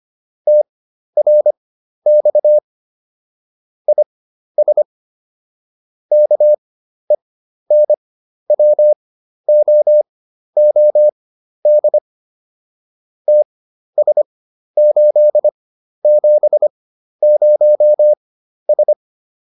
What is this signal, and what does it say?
Morse code, heard as a keyed tone, about 25 wpm, spaced as for 12 wpm, 600 Hz. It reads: TRX IS KENWOOD TS870S